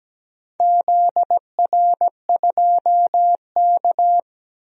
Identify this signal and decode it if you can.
Morse code, heard as a keyed tone, about 17 wpm, 700 Hz. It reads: ZR2K
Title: DXCC-100 Call Signs - 4 Characters - 17wpm